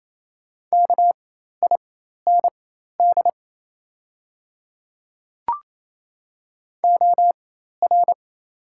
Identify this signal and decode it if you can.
Morse code, heard as a keyed tone, about 27 wpm, spaced as for 13 wpm, 690 Hz. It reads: KIND OR
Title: 4x Character Spacing - Top 200 Words - 28wpm